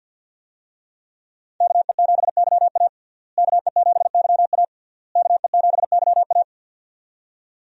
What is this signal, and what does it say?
Morse code, heard as a keyed tone, about 52 wpm, 700 Hz. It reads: KE7YA KE7YA KE7YA
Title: USA Call Signs - 5 Characters - Repeated 3x - 50wpm